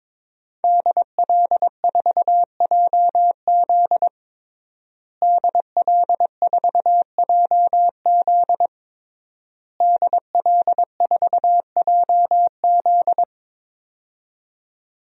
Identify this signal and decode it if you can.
Morse code, heard as a keyed tone, about 22 wpm, 700 Hz. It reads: DL4JZ DL4JZ DL4JZ
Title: DXCC-100 Call Signs - 5 Characters - Repeated 3x - 22wpm